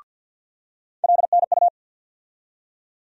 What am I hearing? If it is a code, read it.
Morse code, heard as a keyed tone, about 47 wpm, 700 Hz. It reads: LNU
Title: USA Call Sign Postfixes - 3 Letters 50wpm